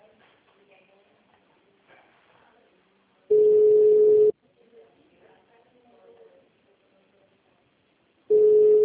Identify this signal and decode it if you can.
Ringback tone (Europe, 425 Hz), with speech.
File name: out- -2010- - - .18